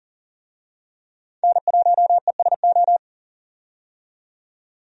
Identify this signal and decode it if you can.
Morse code, heard as a keyed tone, about 40 wpm, 700 Hz. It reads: N1ESO
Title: USA Call Signs - 5 Characters - 40wpm